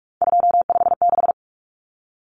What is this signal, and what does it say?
Morse code, heard as a keyed tone, about 44 wpm, 700 Hz. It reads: 256
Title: Common Cut Numbers - 3-Digit 45wpm